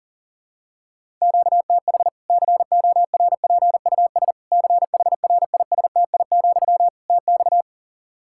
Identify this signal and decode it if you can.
Morse code, heard as a keyed tone, about 40 wpm, 700 Hz. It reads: QTH CORPUS CHRISTI, TX
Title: K1USN SST QSO Practice 40wpm